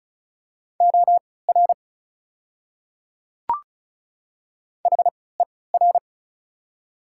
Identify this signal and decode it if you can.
Morse code, heard as a keyed tone, about 34 wpm, spaced as for 20 wpm, 690 Hz. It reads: OR HER
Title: 3x Character Spacing - Top 100 Words - 35wpm